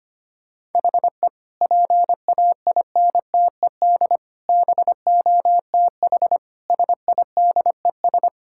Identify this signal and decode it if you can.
Morse code, heard as a keyed tone, about 25 wpm, 700 Hz. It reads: HE PAINTED BOTH SIDES
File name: practice / 25wpm / Sentences with 4-Word Phrases - 25wpm